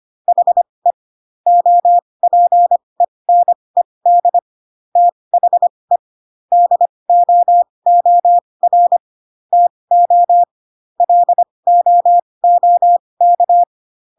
Morse code, heard as a keyed tone, about 25 wpm, spaced as for 20 wpm, 700 Hz. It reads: HE OPENED THE DOOR TO LOOK
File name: Pinocchio / Pinocchio ch1 15 25 ik1hge